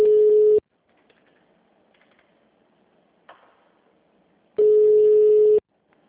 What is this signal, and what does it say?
Ringback tone (Europe, 425 Hz).
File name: out- -2003- - - .76